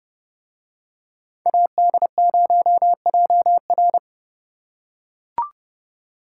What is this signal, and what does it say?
Morse code, heard as a keyed tone, about 30 wpm, 700 Hz. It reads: AD0JR